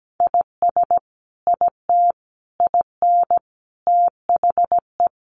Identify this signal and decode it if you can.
Morse code, heard as a keyed tone, about 17 wpm, 700 Hz. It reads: IS IT IN THE
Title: Sentences from Top 1300 Words 17wpm